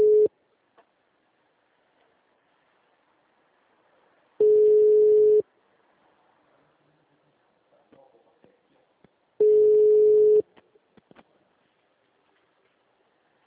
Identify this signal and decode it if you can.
Ringback tone (Europe, 425 Hz).